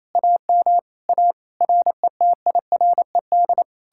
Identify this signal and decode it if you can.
Morse code, heard as a keyed tone, about 37 wpm, 700 Hz. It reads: AM A RETIRED